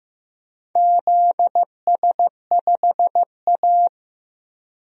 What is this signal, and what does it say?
Morse code, heard as a keyed tone, about 15 wpm, 700 Hz. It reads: ZS5A
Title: DXCC-100 Call Signs - 4 Characters - 15wpm